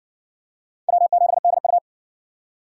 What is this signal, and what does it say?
Morse code, heard as a keyed tone, about 53 wpm, 700 Hz. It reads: W7DU